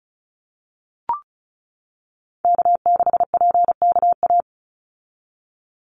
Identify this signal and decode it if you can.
Morse code, heard as a keyed tone, about 34 wpm, 700 Hz. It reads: K6PKA
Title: USA Call Signs 35wpm